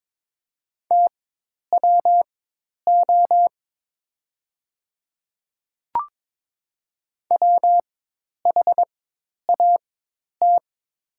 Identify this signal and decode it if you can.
Morse code, heard as a keyed tone, about 22 wpm, spaced as for 10 wpm, 700 Hz. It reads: TWO WHAT
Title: 4x Character Spacing - Top 100 Words - 22wpm